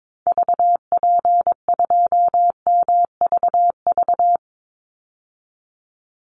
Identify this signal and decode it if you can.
Morse code, heard as a keyed tone, about 22 wpm, 700 Hz. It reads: VP2MVV
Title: Call Signs Repeated 2x - 22wpm